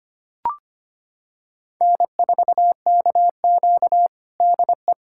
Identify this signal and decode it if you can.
Morse code, heard as a keyed tone, about 25 wpm, 700 Hz. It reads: N4KQ DE
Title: USA Callsign DE Callsign - 4 Characters - 25wpm